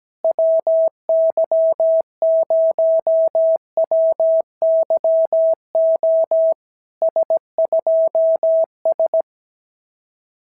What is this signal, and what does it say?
Morse code, heard as a keyed tone, about 17 wpm, 650 Hz. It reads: WY0WYO S2S
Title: SOTA QSO Practice 17wpm